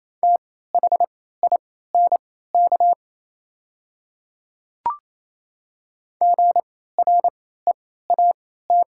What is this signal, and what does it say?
Morse code, heard as a keyed tone, about 28 wpm, spaced as for 16 wpm, 700 Hz. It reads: THINK GREAT
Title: 3x Character Spacing - Top 200 Words - 28wpm